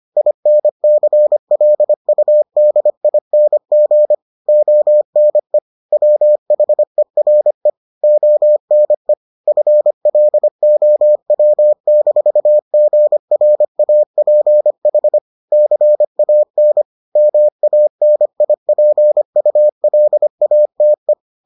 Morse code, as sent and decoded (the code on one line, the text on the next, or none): .. -. -.-. .-.. ..- -.. .. -. --. / --- -. / .-- .... .-. / --- -. / ..-. .-.. --- .-- -....- --. .-. .- .--. .... / -.-. .- -. / -- .- -. .. .--. ..- .-.. .- - .
INCLUDING ONE WHERE ONE FLOW-GRAPH CAN MANIPULATE